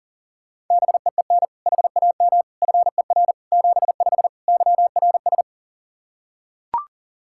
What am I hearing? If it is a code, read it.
Morse code, heard as a keyed tone, about 40 wpm, 700 Hz. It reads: BEEN HAM FER 75 YRS
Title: Practice QSO - Exchange 3 - More Info - 40wpm